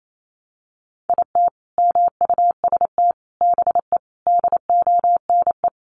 Morse code, heard as a keyed tone, about 28 wpm, 700 Hz. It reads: IT MUST BE DONE
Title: Sentences from Top 1100 Words 28wpm